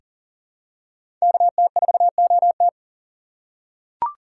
Morse code, heard as a keyed tone, about 40 wpm, 700 Hz. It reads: KT4OT
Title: USA Call Signs - 5 Characters - 2x Character Spacing - 40wpm